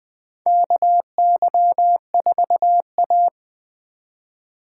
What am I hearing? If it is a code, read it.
Morse code, heard as a keyed tone, about 21 wpm, 700 Hz. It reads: KY4A
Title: USA Call Signs - 4 Characters - 3x Character Spacing - 20wpm